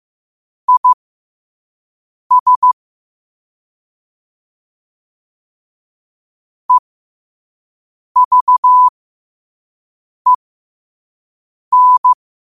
Morse code, heard as a keyed tone, about 15 wpm, spaced as for 5 wpm, 1000 Hz. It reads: IS EVEN
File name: qotdfiles / 5wpm